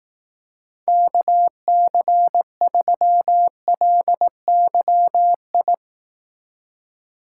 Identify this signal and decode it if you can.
Morse code, heard as a keyed tone, about 18 wpm, 700 Hz. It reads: KC3LYI
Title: Speed Racing - USA 2x3 Call Signs 15wpm